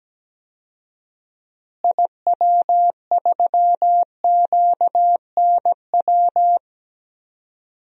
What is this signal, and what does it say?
Morse code, heard as a keyed tone, about 17 wpm, 700 Hz. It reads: IW3QNW
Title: DXCC-100 Call Signs - 6 Characters - Repeated 3x - 17wpm